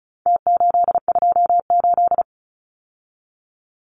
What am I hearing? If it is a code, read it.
Morse code, heard as a keyed tone, about 34 wpm, 700 Hz. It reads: T828
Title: Numbers - Contest Style - 4-Digit 35wpm